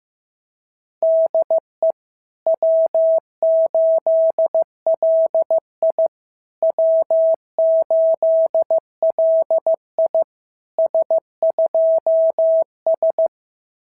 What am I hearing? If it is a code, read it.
Morse code, heard as a keyed tone, about 14 wpm, 650 Hz. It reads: DE W8LI W8LI S2S
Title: SOTA QSO Practice 15wpm